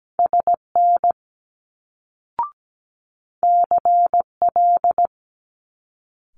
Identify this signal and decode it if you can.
Morse code, heard as a keyed tone, about 17 wpm, 700 Hz. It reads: SN CL
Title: Multi-letter Phonemes Quick Fire 17wpm